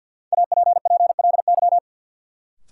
Morse code, heard as a keyed tone, about 48 wpm, 700 Hz. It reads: APPLY
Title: Sentences from Top 1200 Words - New Word Drill Alt - 50wpm